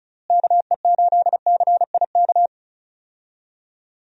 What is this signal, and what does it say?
Morse code, heard as a keyed tone, about 34 wpm, 700 Hz. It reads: KE8CIK
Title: USA Call Signs - 6 Characters - Repeated 3x - 35wpm